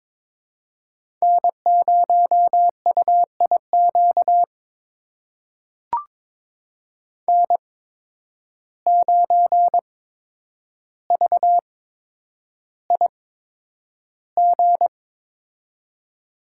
Morse code, heard as a keyed tone, about 22 wpm, spaced as for 9 wpm, 700 Hz. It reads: N0UIQ N9VIG
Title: USA Call Signs - 5 Characters - 8x Character Spacing - 22wpm